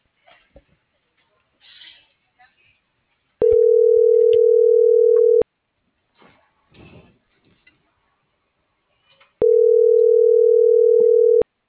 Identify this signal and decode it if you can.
Ringback tone (North America), 440 + 480 Hz.